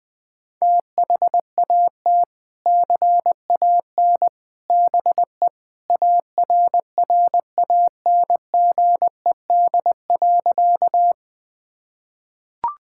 Morse code, heard as a keyed tone, about 21 wpm, 700 Hz. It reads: THAT CAN BE ARRANGED.
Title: Sentences with 3-Word Phrases - 20wpm